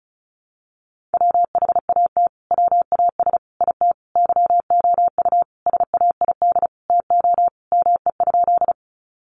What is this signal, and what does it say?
Morse code, heard as a keyed tone, about 34 wpm, 700 Hz. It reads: WHAT WAS IT YOU SAID TO ME?
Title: Sentences from Top 200 Words 35wpm